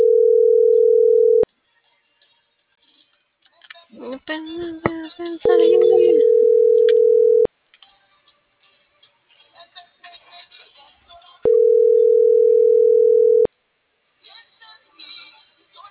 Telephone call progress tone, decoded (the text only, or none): ringback tone